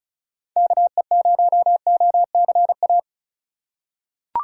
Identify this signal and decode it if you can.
Morse code, heard as a keyed tone, about 34 wpm, 700 Hz. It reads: KE0OCA
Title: USA Call Signs - 2x Character Spacing - 35wpm